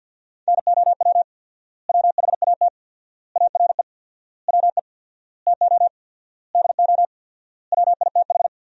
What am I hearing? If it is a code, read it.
Morse code, heard as a keyed tone, about 47 wpm, 700 Hz. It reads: NOW WHAT ARE WE TO DO WITH
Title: 4x Word-Spacing - Sentences from Top 100 Words - 50wpm